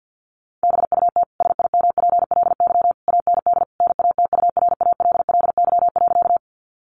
Morse code, heard as a keyed tone, about 50 wpm, 700 Hz. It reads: BUT SIMPLY AND NATURALLY.
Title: Sentences from Top 1300 Words 50wpm